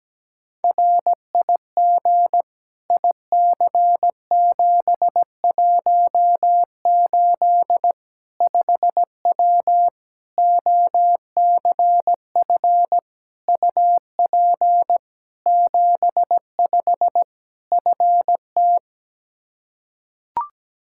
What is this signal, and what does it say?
Morse code, heard as a keyed tone, about 17 wpm, 700 Hz. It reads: RIG IC718 5W OCF UP 75 FT
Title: K1USN SST QSO Practice 17wpm